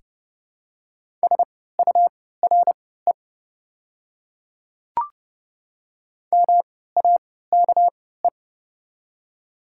Morse code, heard as a keyed tone, about 31 wpm, spaced as for 17 wpm, 700 Hz. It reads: SURE MAKE
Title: 3x Character Spacing - Top 300 Words - 30wpm